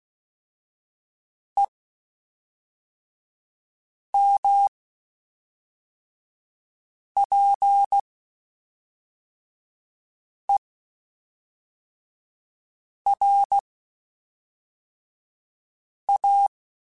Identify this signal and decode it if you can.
Morse code, heard as a keyed tone, about 16 wpm, 800 Hz. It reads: E M P E R A
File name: Spaced Characters / Spaced Characters-mp3s / Weather5 1 sp.@5